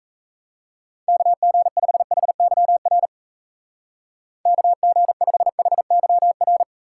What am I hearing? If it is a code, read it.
Morse code, heard as a keyed tone, about 35 wpm, 700 Hz. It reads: KG5HYR KG5HYR
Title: Speed Racing - USA 2x3 Call Signs 28wpm